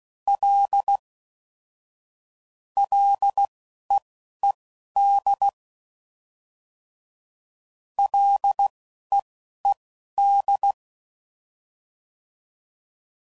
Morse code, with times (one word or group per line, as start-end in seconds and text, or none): 0.3-1.0 L
2.8-5.5 LEED
8.0-10.7 LEED